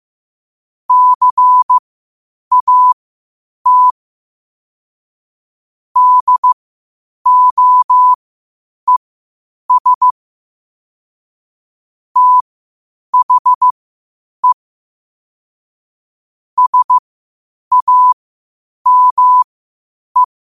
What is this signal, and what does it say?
Morse code, heard as a keyed tone, about 15 wpm, spaced as for 8 wpm, 1000 Hz. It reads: CAT DOES THE SAME
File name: qotdfiles / 8wpm